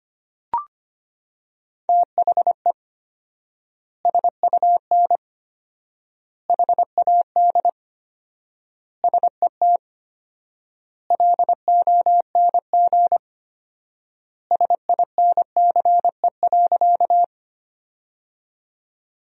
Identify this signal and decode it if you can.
Morse code, heard as a keyed tone, about 25 wpm, 700 Hz. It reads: THE SUN HAD SET LONG SINCE.
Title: 4x Word-Spacing - Sentences from Top 300 Words - 25wpm